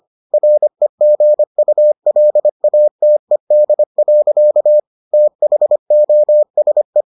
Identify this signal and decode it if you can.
Morse code, heard as a keyed tone, about 25 wpm, 600 Hz. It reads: REGULATED. THOSE